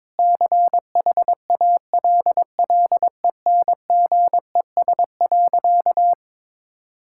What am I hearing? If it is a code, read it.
Morse code, heard as a keyed tone, about 22 wpm, 700 Hz. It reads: CHALLENGES.
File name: Sentences with 7-Word Phrases - 22wpm